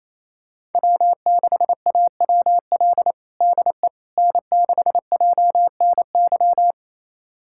Morse code, heard as a keyed tone, about 28 wpm, 700 Hz. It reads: W6AWL DE N6JNY
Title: USA Callsign DE Callsign - 5 Characters - 28wpm